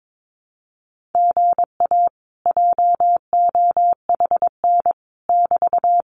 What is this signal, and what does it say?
Morse code, heard as a keyed tone, about 22 wpm, 700 Hz. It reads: GA JOHN <BT>
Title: Common QSO Phrases Encore 22wpm